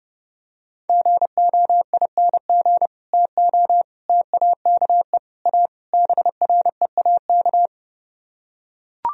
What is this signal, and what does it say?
Morse code, heard as a keyed tone, about 31 wpm, 700 Hz. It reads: GOING TO TAKE A BREAK